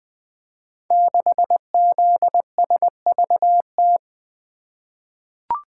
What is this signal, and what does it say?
Morse code, heard as a keyed tone, about 20 wpm, 700 Hz. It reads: 6ZSVT
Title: Code Groups - 5 Letter:Numbers - 20wpm